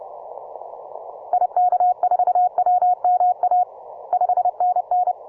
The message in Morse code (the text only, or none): IK4WMA 5NN